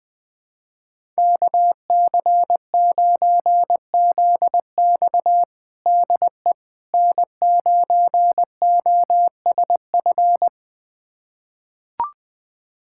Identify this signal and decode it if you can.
Morse code, heard as a keyed tone, about 20 wpm, 700 Hz. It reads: KC9ZX DE N9OSF
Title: USA Callsign DE Callsign - 5 Characters - 20wpm